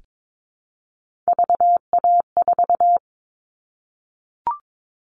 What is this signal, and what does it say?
Morse code, heard as a keyed tone, about 22 wpm, 700 Hz. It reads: VA4
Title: All Cut Numbers - 3-Digit 22wpm